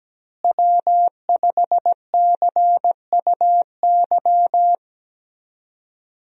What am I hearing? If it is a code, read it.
Morse code, heard as a keyed tone, about 17 wpm, 700 Hz. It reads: W5CUY
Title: USA Call Signs - 5 Characters - Repeated 2x - 17wpm